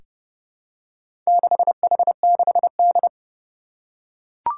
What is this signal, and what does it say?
Morse code, heard as a keyed tone, about 30 wpm, 700 Hz. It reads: 6H6D
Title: Four Challenging Letter-Number Combinations 30wpm